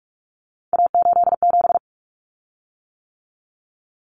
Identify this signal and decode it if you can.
Morse code, heard as a keyed tone, about 44 wpm, 700 Hz. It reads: A87
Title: Common Cut Numbers - 3-Digit 45wpm